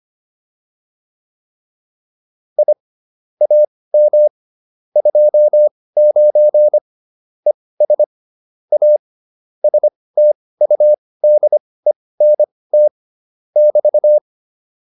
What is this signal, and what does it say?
Morse code, heard as a keyed tone, about 25 wpm, spaced as for 18 wpm, 600 Hz. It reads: I AM 29 ES A STUDENT <BT>